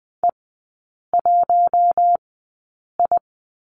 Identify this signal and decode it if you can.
Morse code, heard as a keyed tone, about 19 wpm, spaced as for 8 wpm, 700 Hz. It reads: E1I
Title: 067 Lesson 12e - Letter d - Three Characters 20wpm